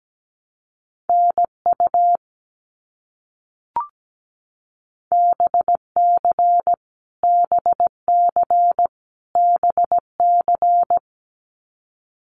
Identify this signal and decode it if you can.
Morse code, heard as a keyed tone, about 17 wpm, 700 Hz. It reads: NU BC BC BC
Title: Canadian Provinces and Territories - Repeated 3x - 17wpm